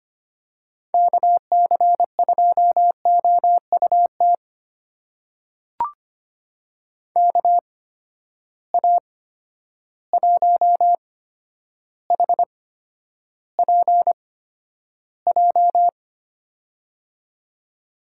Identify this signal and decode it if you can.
Morse code, heard as a keyed tone, about 25 wpm, spaced as for 11 wpm, 700 Hz. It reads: KC2OUT KA1HPJ
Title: USA Call Signs - 6 Characters - 8x Character Spacing - 25wpm